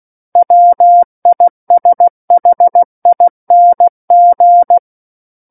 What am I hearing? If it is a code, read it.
Morse code, heard as a keyed tone, about 16 wpm, 700 Hz. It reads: WISHING
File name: Book One / 07-Chapter 7 - 16 WPM